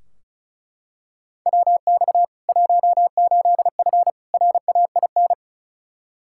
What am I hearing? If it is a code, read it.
Morse code, heard as a keyed tone, about 36 wpm, 700 Hz. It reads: WX 18F RAIN
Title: Practice QSO - Exchange 2 - Info - 35wpm